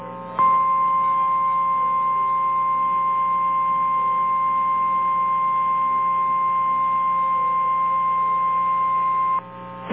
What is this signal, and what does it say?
Emergency Alert System (SAME) tones (the 1050 Hz attention tone). It is heard from 0.3 to 9.4 s.
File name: eas / KNCYKBIE / KNCYKBIE